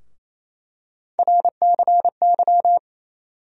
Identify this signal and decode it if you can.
Morse code, heard as a keyed tone, about 28 wpm, 700 Hz. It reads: RCY
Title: USA Call Sign Postfixes - 3 Letters 28wpm